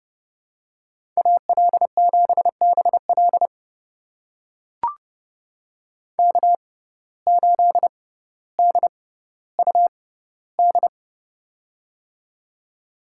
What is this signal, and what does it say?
Morse code, heard as a keyed tone, about 30 wpm, spaced as for 15 wpm, 700 Hz. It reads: AL7BL K8DUD
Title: USA Call Signs - 5 Characters - 6x Character Spacing - 30wpm